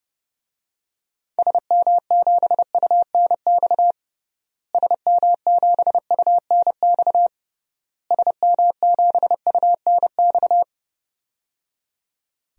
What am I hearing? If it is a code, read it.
Morse code, heard as a keyed tone, about 30 wpm, 700 Hz. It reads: SM7UNX SM7UNX SM7UNX
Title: DXCC-100 Call Signs - 6 Characters - Repeated 3x - 30wpm